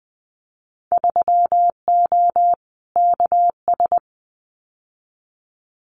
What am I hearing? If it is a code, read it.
Morse code, heard as a keyed tone, about 20 wpm, 700 Hz. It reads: 3O KS
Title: Winter Field Day Exchanges Encore 20wpm